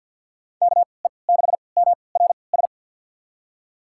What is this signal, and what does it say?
Morse code, heard as a keyed tone, about 46 wpm, spaced as for 27 wpm, 690 Hz. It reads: KE6MRS
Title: USA Call Signs - 6 Characters - 3x Character Spacing - 50wpm